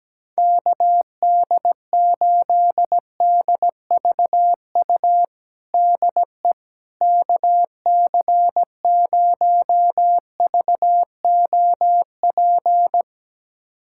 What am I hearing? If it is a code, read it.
Morse code, heard as a keyed tone, about 17 wpm, 700 Hz. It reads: KD8DVU DE KC0VOP
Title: USA Callsign DE Callsign 17wpm